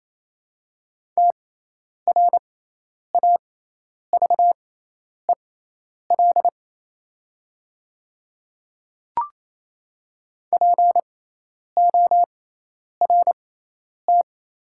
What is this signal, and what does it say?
Morse code, heard as a keyed tone, about 28 wpm, spaced as for 10 wpm, 700 Hz. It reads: TRAVEL PORT
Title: 6x Character Spacing - Top 500 Words - 28wpm